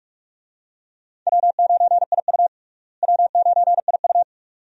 Morse code, heard as a keyed tone, about 42 wpm, 700 Hz. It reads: W9IU W9IU
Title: USA Call Signs - 4 Characters - Repeated 2x - 45wpm